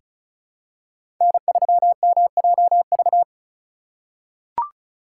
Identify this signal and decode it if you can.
Morse code, heard as a keyed tone, about 36 wpm, 700 Hz. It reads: N3MJV
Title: USA Call Signs - 5 Characters - 3x Character Spacing - 35wpm